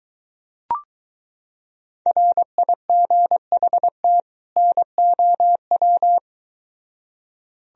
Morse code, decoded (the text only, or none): RIGHT NOW